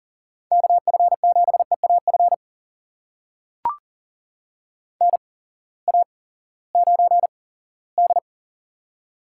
Morse code, heard as a keyed tone, about 40 wpm, spaced as for 20 wpm, 700 Hz. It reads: KF7EAF NA9D
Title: USA Call Signs - 8x Character Spacing - 40wpm